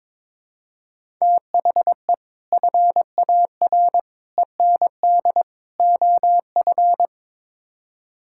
Morse code, heard as a keyed tone, about 22 wpm, 700 Hz. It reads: THE FAR END OF
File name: practice / 22wpm / Build A Phrase - 5-Word Phrases - 22wpm